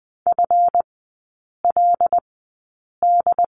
Morse code, heard as a keyed tone, about 20 wpm, spaced as for 8 wpm, 700 Hz. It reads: FLD
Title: 191 Lesson 31e - Letter v - Three Characters 20wpm